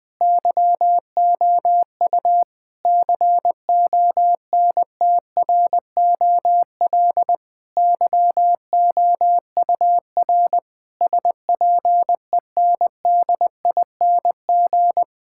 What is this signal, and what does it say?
Morse code, heard as a keyed tone, about 20 wpm, 700 Hz. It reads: YOU CONTROL YOUR SPENDING
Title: Difficult Sentences with Hints - 20wpm